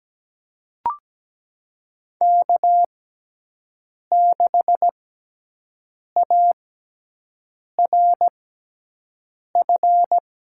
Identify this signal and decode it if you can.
Morse code, heard as a keyed tone, about 17 wpm, spaced as for 6 wpm, 700 Hz. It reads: K6ARF
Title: USA Call Signs - 6x Character Spacing - 17wpm